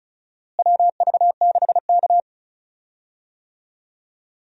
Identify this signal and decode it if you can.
Morse code, heard as a keyed tone, about 36 wpm, 700 Hz. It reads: WV6K
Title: USA Call Signs - 4 Characters - Repeated 2x - 35wpm